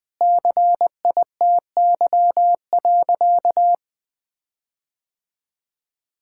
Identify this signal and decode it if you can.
Morse code, heard as a keyed tone, about 21 wpm, 700 Hz. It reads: CITY.
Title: Sentences with 6-Word Phrases - Spoken Beginning - 20wpm